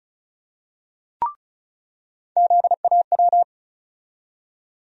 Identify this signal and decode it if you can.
Morse code, heard as a keyed tone, about 36 wpm, 700 Hz. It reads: ZAW
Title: USA Call Sign Postfixes - 3 Letters 35wpm